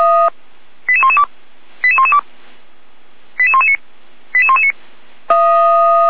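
Five-tone selective call (ZVEI), heard as quick runs of five tones, tones of about 70 ms. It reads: 88112 88112 88119 88119
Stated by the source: ZVEI five-tone selective call received on an SDR.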